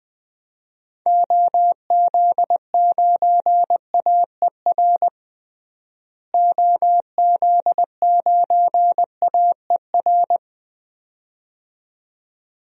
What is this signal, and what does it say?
Morse code, heard as a keyed tone, about 20 wpm, 700 Hz. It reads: OZ9AER OZ9AER
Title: DXCC-100 Call Signs - 6 Characters - Repeated 2x - 20wpm